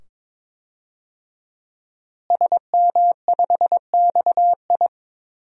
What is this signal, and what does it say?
Morse code, heard as a keyed tone, about 22 wpm, 700 Hz. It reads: SM5XI
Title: DXCC-100 Call Signs - 5 Characters - Repeated 2x - 22wpm